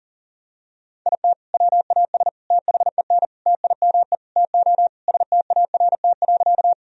Morse code, heard as a keyed tone, about 40 wpm, 700 Hz. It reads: IT WAS THEN TIME TO START.
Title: Sentences from Top 1000 Words - Review Part 1 - 40wpm